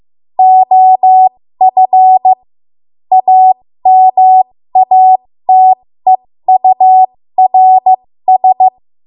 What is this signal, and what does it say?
Morse code, heard as a keyed tone, about 15 wpm, spaced as for 13 wpm, 750 Hz. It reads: OF AMATEURS
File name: Morse / 13WPM